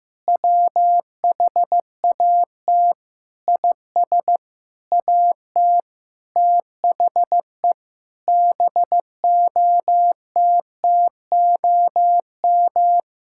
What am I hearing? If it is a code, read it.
Morse code, heard as a keyed tone, about 15 wpm, 700 Hz. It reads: WHAT IS AT THE BOTTOM
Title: Sentences from Top 1000 Words - Review Part 6 - 15wpm